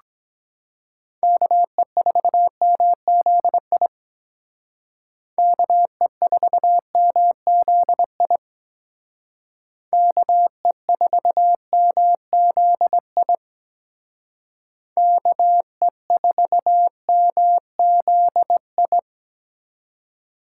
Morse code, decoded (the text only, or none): KE4MZI KE4MZI KE4MZI KE4MZI